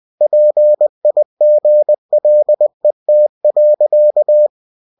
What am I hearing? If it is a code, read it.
Morse code, heard as a keyed tone, about 20 wpm, 600 Hz. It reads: PIGLET.
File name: cw-books / winnie-the-pooh / winnie-20wpm-page-0073